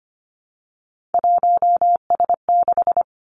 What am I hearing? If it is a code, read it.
Morse code, heard as a keyed tone, about 25 wpm, 700 Hz. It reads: 1S6
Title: Three Challenging Letter-Number Combinations 25wpm